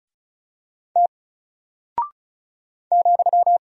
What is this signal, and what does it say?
Morse code, heard as a keyed tone, about 36 wpm, 700 Hz. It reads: T ,
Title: Single Character (Letter:Number:Punctuation:Prosign) Alt 35wpm